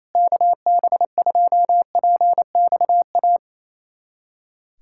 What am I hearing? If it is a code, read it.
Morse code, heard as a keyed tone, about 28 wpm, 700 Hz. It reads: KB2PXA
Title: USA Callsign DE Callsign - 6 Characters - 28wpm